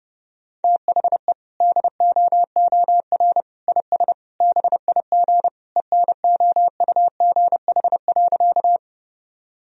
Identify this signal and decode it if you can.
Morse code, heard as a keyed tone, about 28 wpm, 700 Hz. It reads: THE DOOR IS BIG ENOUGH.
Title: Sentences from Top 1000 Words - Review Part 1 - 30wpm